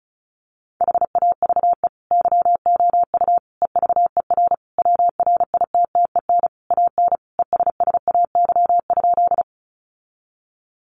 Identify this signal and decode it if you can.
Morse code, heard as a keyed tone, about 34 wpm, 700 Hz. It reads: HAVE YOU EVER WRITTEN AN ESSAY?
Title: Sentences from Top 1500 Words 35wpm